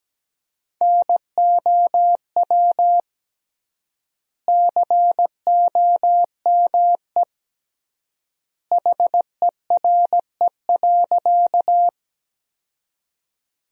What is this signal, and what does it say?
Morse code, heard as a keyed tone, about 17 wpm, 700 Hz. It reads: NOW COME HERE.
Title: 3x Word-Spacing - Sentences from Top 200 Words - 17wpm